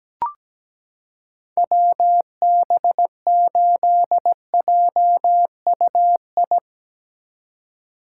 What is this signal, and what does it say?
Morse code, heard as a keyed tone, about 17 wpm, 700 Hz. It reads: WB8JUI
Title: K1USN SST QSO Practice - 17wpm